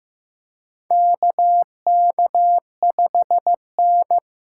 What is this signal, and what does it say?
Morse code, heard as a keyed tone, about 15 wpm, 700 Hz. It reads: KK5N
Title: USA Call Signs - 3x Character Spacing - 15wpm